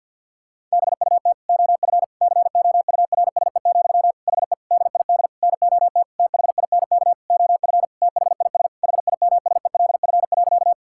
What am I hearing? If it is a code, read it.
Morse code, heard as a keyed tone, about 50 wpm, 700 Hz. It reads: BUT OF COURSE, HE DID NOT THINK OF THIS HIMSELF.
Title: Sentences from Top 1600 Words 50wpm